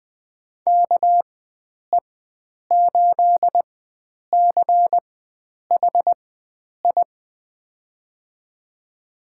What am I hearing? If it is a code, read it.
Morse code, heard as a keyed tone, about 21 wpm, spaced as for 9 wpm, 700 Hz. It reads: KE8CHI
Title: USA Call Signs - 6 Characters - 4x Character Spacing - 20wpm